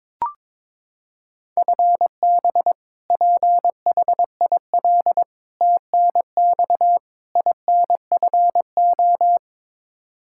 Morse code, heard as a keyed tone, about 22 wpm, 700 Hz. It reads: FB PHIL TNX INFO
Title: Practice QSO - Exchange 3 - More Info - 22wpm